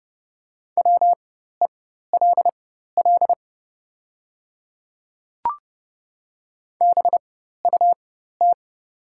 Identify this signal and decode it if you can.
Morse code, heard as a keyed tone, about 30 wpm, spaced as for 14 wpm, 700 Hz. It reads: WELL BUT
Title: 4x Character Spacing - Top 200 Words - 30wpm